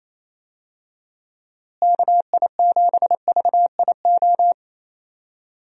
Morse code, heard as a keyed tone, about 28 wpm, 700 Hz. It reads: KI7VIO
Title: USA Call Signs - 6 Characters - Repeated 3x - 28wpm